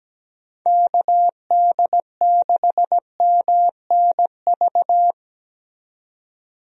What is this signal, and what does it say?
Morse code, heard as a keyed tone, about 17 wpm, 700 Hz. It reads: KD6MNV
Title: USA Call Signs - 6 Characters - 17wpm